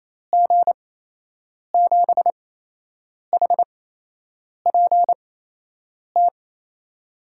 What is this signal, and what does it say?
Morse code, heard as a keyed tone, about 28 wpm, spaced as for 8 wpm, 700 Hz. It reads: G7HPT
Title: USA Call Signs - 6 Characters - 8x Character Spacing - 28wpm